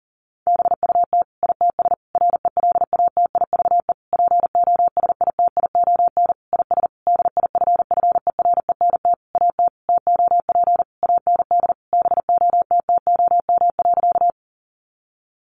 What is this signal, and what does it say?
Morse code, heard as a keyed tone, about 40 wpm, 700 Hz. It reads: BUT ITS RELATIVE POSITION IS DIFFERENT AT TOP AND BOTTOM.
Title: Sentences from Top 1400 Words 40wpm